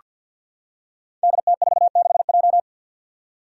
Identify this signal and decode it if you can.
Morse code, heard as a keyed tone, about 49 wpm, 700 Hz. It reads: DT46J
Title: Five Challenging Letter-Number Combinations 50wpm